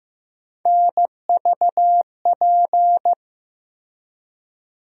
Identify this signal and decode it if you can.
Morse code, heard as a keyed tone, about 15 wpm, 700 Hz. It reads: NVP
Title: USA Call Sign Postfixes - 3 Letters 15wpm